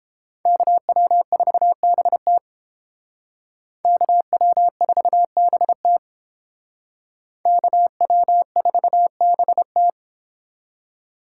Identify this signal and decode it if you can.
Morse code, heard as a keyed tone, about 28 wpm, 700 Hz. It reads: KW4BT KW4BT KW4BT